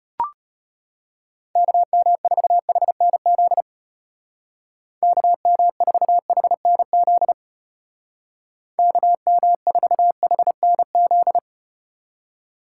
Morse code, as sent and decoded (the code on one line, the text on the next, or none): -.- -- ....- .... -. --.. / -.- -- ....- .... -. --.. / -.- -- ....- .... -. --..
KM4HNZ KM4HNZ KM4HNZ